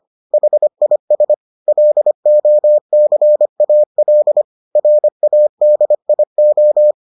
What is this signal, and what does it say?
Morse code, heard as a keyed tone, about 25 wpm, 600 Hz. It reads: HIS LOCAL RADIO